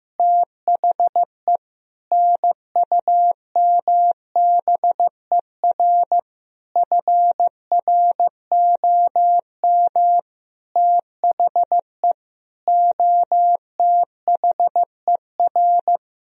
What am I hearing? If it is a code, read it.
Morse code, heard as a keyed tone, about 15 wpm, 700 Hz. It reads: THE NUMBER FROM THE OTHER
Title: Sentences from Top 1000 Words - Review Part 1 - 15wpm